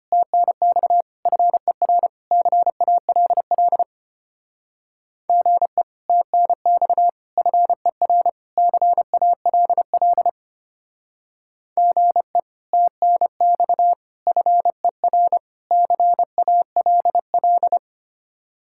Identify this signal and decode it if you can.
Morse code, heard as a keyed tone, about 30 wpm, 700 Hz. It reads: TNX FER CALL GE TNX FER CALL GE TNX FER CALL
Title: Speed-Racing Common QSO Phrases - 25wpm